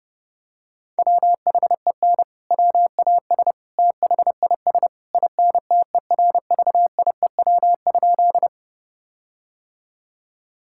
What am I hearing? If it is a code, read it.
Morse code, heard as a keyed tone, about 29 wpm, 700 Hz. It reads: WHEN WAS THIS INTERVIEW?
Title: Sentences from Top 1600 Words 30wpm